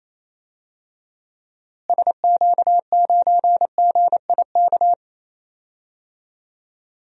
Morse code, heard as a keyed tone, about 28 wpm, 700 Hz. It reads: SQ9GIK
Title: DXCC-100 Call Signs - 6 Characters - Repeated 3x - 28wpm